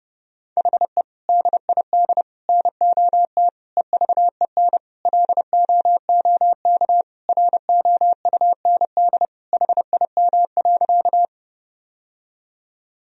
Morse code, heard as a keyed tone, about 30 wpm, 700 Hz. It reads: HE DID NOT EVEN LOOK ROUND HIM.
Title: Sentences from Top 1000 Words - Review Part 1 - 30wpm